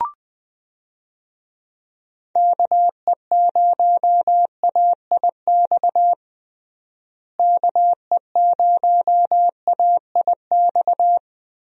Morse code, heard as a keyed tone, about 20 wpm, 700 Hz. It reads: KE0AIX KE0AIX
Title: USA Call Signs - 6 Characters - Repeated 2x - 20wpm